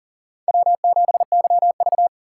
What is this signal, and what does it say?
Morse code, heard as a keyed tone, about 40 wpm, 700 Hz. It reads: W7YV
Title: USA Call Signs - 4 Characters - 3x Character Spacing - 40wpm